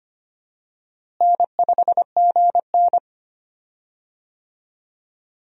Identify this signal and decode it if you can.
Morse code, heard as a keyed tone, about 25 wpm, 700 Hz. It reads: N5GN
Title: USA Call Signs - 4 Characters - Repeated 3x - 25wpm